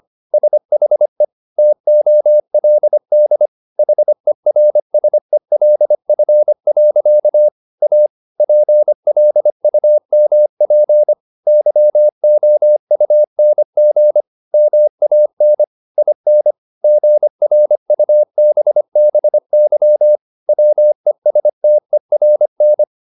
Morse code, heard as a keyed tone, about 25 wpm, 600 Hz. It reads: SHE TOLD HERSELF. A PLUMP YOUNG MAN IN GRUBBY WESTERN